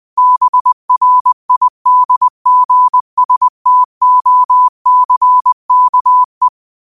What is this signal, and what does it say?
Morse code, heard as a keyed tone, about 20 wpm, 1000 Hz. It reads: BRIDGSTOCKE
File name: qotdfiles / 20wpm